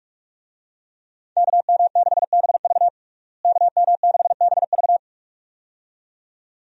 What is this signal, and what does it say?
Morse code, heard as a keyed tone, about 42 wpm, 700 Hz. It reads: KM6BV KM6BV
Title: USA Call Signs - 5 Characters - Repeated 2x - 45wpm